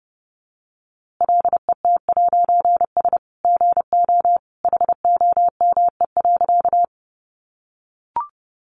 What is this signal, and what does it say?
Morse code, heard as a keyed tone, about 30 wpm, 700 Hz. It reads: LET'S GO HOME.
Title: A Sound Of Thunder 30wpm